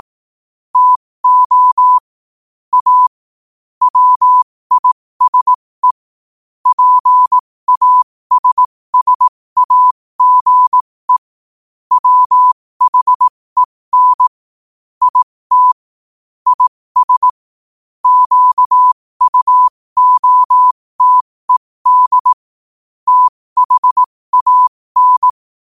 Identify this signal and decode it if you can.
Morse code, heard as a keyed tone, about 18 wpm, spaced as for 15 wpm, 1000 Hz. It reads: TO A WISE PASSAGE WHEN IT IS QUOTED THAN
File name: qotdfiles / 15wpm